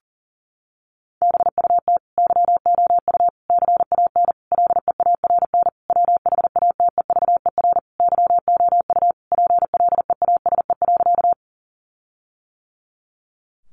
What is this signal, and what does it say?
Morse code, heard as a keyed tone, about 40 wpm, 700 Hz. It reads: BUT YOU CAN LEARN WHATEVER YOU PLEASE.
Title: Sentences from Top 1500 Words 40wpm